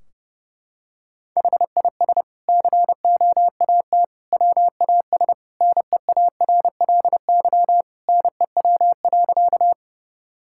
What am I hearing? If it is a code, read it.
Morse code, heard as a keyed tone, about 30 wpm, 700 Hz. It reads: HIS COAT WAS NEARLY NEW.
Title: Sentences from Top 1600 Words 30wpm